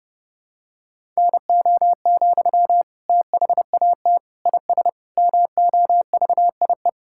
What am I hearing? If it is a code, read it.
Morse code, heard as a keyed tone, about 30 wpm, 700 Hz. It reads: NO, THAT IS MOVIE
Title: Sentences from Top 1000 Words - Review Part 5 - 30wpm